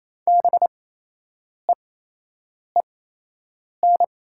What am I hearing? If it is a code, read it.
Morse code, heard as a keyed tone, about 28 wpm, spaced as for 8 wpm, 700 Hz. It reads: BEEN
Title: 8x Character Spacing - Top 100 Words - 28wpm